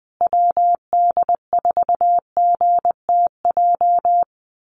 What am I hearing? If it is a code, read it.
Morse code, heard as a keyed tone, about 20 wpm, 700 Hz. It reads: WD4GTJ
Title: USA Call Signs Repeated 2x 20wpm